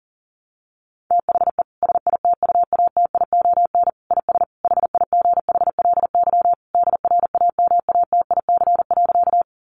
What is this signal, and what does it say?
Morse code, heard as a keyed tone, about 40 wpm, 700 Hz. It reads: THE SITUATION IS HIGHLY DRAMATIC.
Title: Sentences from Top 1300 Words 40wpm